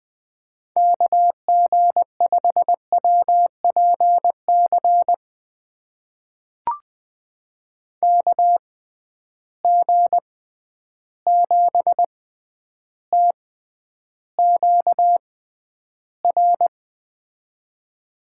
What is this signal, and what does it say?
Morse code, heard as a keyed tone, about 20 wpm, spaced as for 10 wpm, 700 Hz. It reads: KG5WPC KG7TQR
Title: USA Call Signs - 6 Characters - 6x Character Spacing - 20wpm